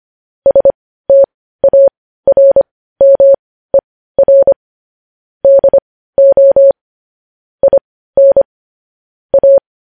Morse code, heard as a keyed tone, about 25 wpm, spaced as for 15 wpm, 550 Hz. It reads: STARMER DO IN A
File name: podcast / News headlines-1x-25x15